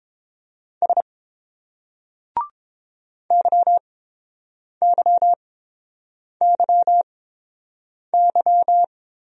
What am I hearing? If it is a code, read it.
Morse code, heard as a keyed tone, about 27 wpm, spaced as for 7 wpm, 700 Hz. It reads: S YYYY
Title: Speed-Racing Single Letters 22wpm